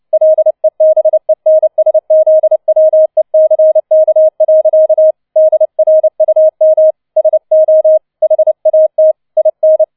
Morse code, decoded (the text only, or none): LEBENSZWECK. DRUM SO HAT IN